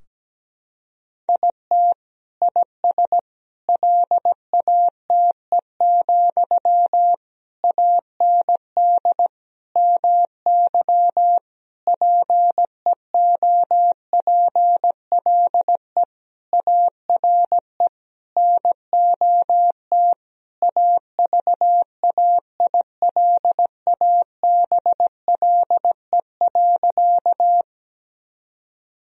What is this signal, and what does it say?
Morse code, heard as a keyed tone, about 17 wpm, 700 Hz. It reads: IT IS LATE, AND MY PEOPLE ARE NOT AVAILABLE.
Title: Sentences from Top 1000 Words - Review Part 3 - 17wpm